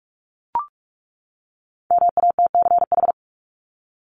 Morse code, heard as a keyed tone, about 44 wpm, 700 Hz. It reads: MATCH
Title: Top 1501-1750 Words 45wpm